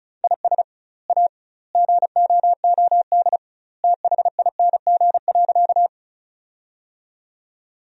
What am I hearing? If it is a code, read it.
Morse code, heard as a keyed tone, about 34 wpm, 700 Hz. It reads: IS A GOOD THING.
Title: 2x Word-Spacing - Sentences from Top 300 Words - 35wpm